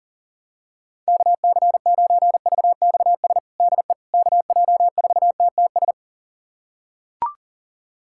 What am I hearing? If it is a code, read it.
Morse code, heard as a keyed tone, about 40 wpm, 700 Hz. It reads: KC9VXS DE KJ4TTS
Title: USA Callsign DE Callsign - 6 Characters - 40wpm